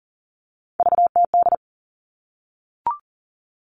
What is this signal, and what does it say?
Morse code, heard as a keyed tone, about 40 wpm, 700 Hz. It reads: VTD
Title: All Cut Numbers - 3-Digit 40wpm